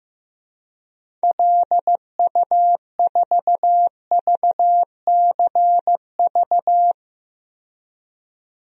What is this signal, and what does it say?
Morse code, heard as a keyed tone, about 15 wpm, 700 Hz. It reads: LU4VCV
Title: DXCC-100 Call Signs - 6 Characters - 15wpm